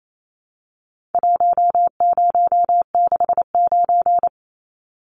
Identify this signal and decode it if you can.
Morse code, heard as a keyed tone, about 28 wpm, 700 Hz. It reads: 1069
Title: Numbers - Contest Style - 4-Digit 28wpm